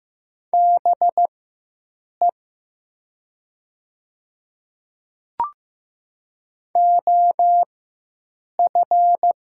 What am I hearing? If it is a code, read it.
Morse code, heard as a keyed tone, about 15 wpm, spaced as for 7 wpm, 700 Hz. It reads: BE OF
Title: 4x Character Spacing - Top 100 Words - 15wpm